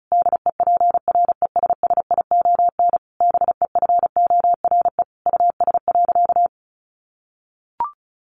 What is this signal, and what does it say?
Morse code, heard as a keyed tone, about 34 wpm, 700 Hz. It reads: DEPRESSION BEFORE US.
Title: Sentences from Top 1300 Words 35wpm